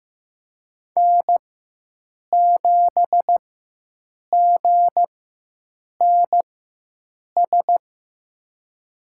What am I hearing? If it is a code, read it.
Morse code, heard as a keyed tone, about 15 wpm, spaced as for 7 wpm, 700 Hz. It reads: N7GNS
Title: USA Call Signs - 4x Character Spacing - 15wpm